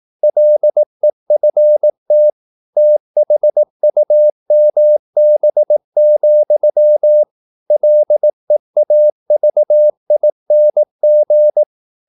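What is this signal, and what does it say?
Morse code, heard as a keyed tone, about 18 wpm, 600 Hz. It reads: LEFT THUMB, LEAVING
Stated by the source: Morse practice at 18 wpm, English text, from CW med Gnister.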